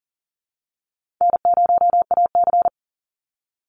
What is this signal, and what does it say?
Morse code, heard as a keyed tone, about 40 wpm, 700 Hz. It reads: N0AC
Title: Easy Call Signs Repeated 2x - 40wpm